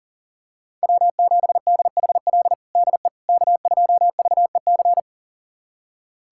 Morse code, heard as a keyed tone, about 40 wpm, 700 Hz. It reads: W7DHL DE K2VEC
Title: USA Callsign DE Callsign - 5 Characters - 40wpm